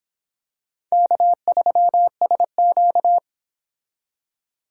Morse code, heard as a keyed tone, about 26 wpm, 700 Hz. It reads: K3SQ